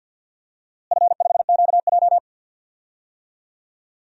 Morse code, heard as a keyed tone, about 50 wpm, 700 Hz. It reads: F5QJ
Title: DXCC-100 Call Signs - 4 Characters - Repeated 3x - 50wpm